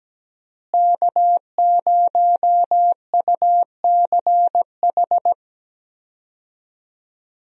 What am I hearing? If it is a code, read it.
Morse code, heard as a keyed tone, about 17 wpm, 700 Hz. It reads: K0UCH
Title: USA Call Signs - 5 Characters - Repeated 3x - 17wpm